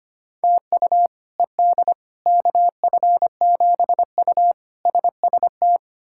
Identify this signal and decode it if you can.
Morse code, heard as a keyed tone, about 25 wpm, 700 Hz. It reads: TU ED KF7U SST
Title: K1USN SST QSO Practice - 25wpm